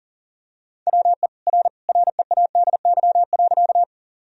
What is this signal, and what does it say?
Morse code, heard as a keyed tone, about 40 wpm, 700 Hz. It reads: WE R READY.
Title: Sentences from Top 400 Words and Abbreviations 40wpm